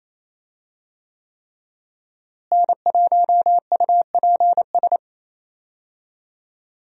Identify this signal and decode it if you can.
Morse code, heard as a keyed tone, about 28 wpm, 700 Hz. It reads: N1UPS
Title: USA Call Signs - 5 Characters - Repeated 3x - 28wpm